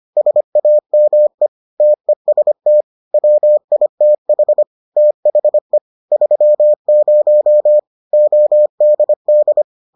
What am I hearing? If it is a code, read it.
Morse code, heard as a keyed tone, about 25 wpm, 600 Hz. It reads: SAME TEST WITH THE 30 ODD